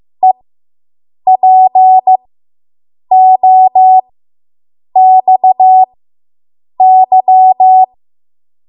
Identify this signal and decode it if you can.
Morse code, heard as a keyed tone, about 15 wpm, spaced as for 7 wpm, 750 Hz. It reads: EPOXY